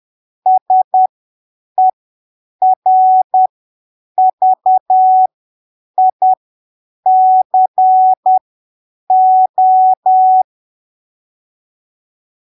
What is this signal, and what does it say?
Morse code, heard as a keyed tone, about 10 wpm, spaced as for 7 wpm, 750 Hz. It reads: SERVICO
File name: arquivos / texto / texto 01 10ppm